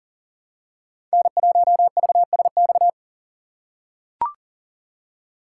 Morse code, heard as a keyed tone, about 40 wpm, 700 Hz. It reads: N1VSX
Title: USA Call Signs - 5 Characters - 2x Character Spacing - 40wpm